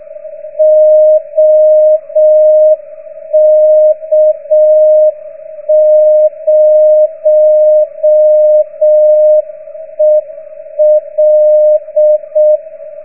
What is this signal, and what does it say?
Morse code, heard as a keyed tone, about 6 wpm, 610 Hz. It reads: OK0EL